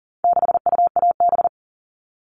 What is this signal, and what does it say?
Morse code, heard as a keyed tone, about 40 wpm, 700 Hz. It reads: 6UAB
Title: All Cut Numbers - 4-Digit 40wpm